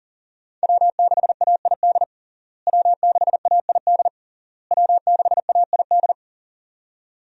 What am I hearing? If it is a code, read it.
Morse code, heard as a keyed tone, about 40 wpm, 700 Hz. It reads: W6AID W6AID W6AID
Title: USA Call Signs - 5 Characters - Repeated 3x - 40wpm